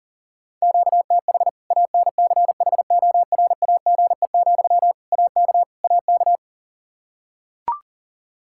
Morse code, heard as a keyed tone, about 40 wpm, 700 Hz. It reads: QTH ANCHORAGE, AK AK
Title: K1USN SST QSO Practice 40wpm